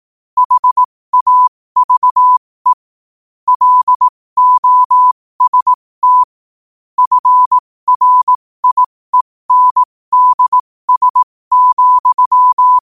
Morse code, heard as a keyed tone, about 18 wpm, spaced as for 15 wpm, 1000 Hz. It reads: HAVE LOST FRIENDS,
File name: qotdfiles / 15wpm